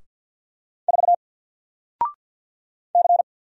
Morse code, heard as a keyed tone, about 48 wpm, 700 Hz. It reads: 4 C
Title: Single Letter-Number Alt 50wpm